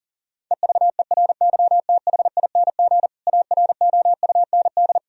Morse code, heard as a keyed tone, about 40 wpm, 700 Hz. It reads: EVERYTHING AROUND
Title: Sentences from Top 1000 Words - Review Part 3 - 40wpm